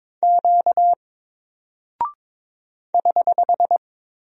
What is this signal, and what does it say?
Morse code, heard as a keyed tone, about 22 wpm, 700 Hz. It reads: Q <HH>
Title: Single Character (Letter:Number:Punctuation:Prosign) Alt 22wpm